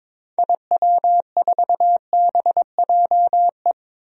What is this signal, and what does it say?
Morse code, heard as a keyed tone, about 22 wpm, 700 Hz. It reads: IW4BJE
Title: DXCC-100 Call Signs - 6 Characters - Repeated 2x - 22wpm